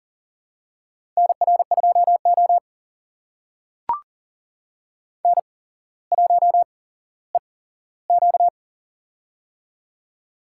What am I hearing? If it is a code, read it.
Morse code, heard as a keyed tone, about 38 wpm, spaced as for 17 wpm, 700 Hz. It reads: NR2O N1EQ
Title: USA Call Signs - 4 Characters - 8x Character Spacing - 40wpm